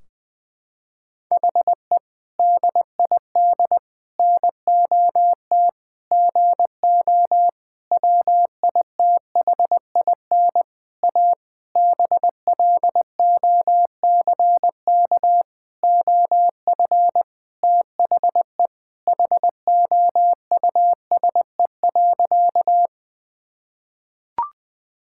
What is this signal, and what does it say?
Morse code, heard as a keyed tone, about 20 wpm, 700 Hz. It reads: HE DID NOT GO WITHIN A BLOCK OF THE HOUSE.
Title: Sentences from Top 1600 Words 20wpm